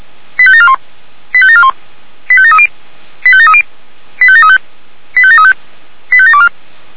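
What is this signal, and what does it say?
Five-tone selective call (ZVEI), heard as quick runs of five tones, tones of about 70 ms. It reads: 85741 85741 85730 85730 85735 85735 85724
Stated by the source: ZVEI five-tone selective call 85745, received on an SDR.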